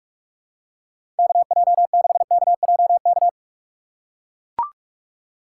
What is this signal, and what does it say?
Morse code, heard as a keyed tone, about 42 wpm, 700 Hz. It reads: KJ6KJK
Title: USA Call Signs - 6 Characters - 6x Character Spacing - 45wpm